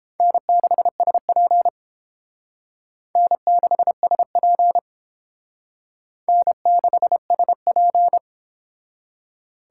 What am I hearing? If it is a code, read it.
Morse code, heard as a keyed tone, about 32 wpm, 700 Hz. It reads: N6SP N6SP N6SP